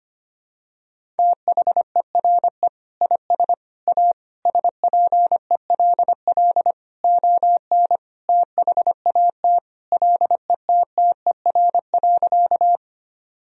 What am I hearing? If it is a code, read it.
Morse code, heard as a keyed tone, about 25 wpm, 700 Hz. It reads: THERE IS A SPELL ON THAT LETTER.
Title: Sentences from Top 1000 Words - Review Part 1 - 25wpm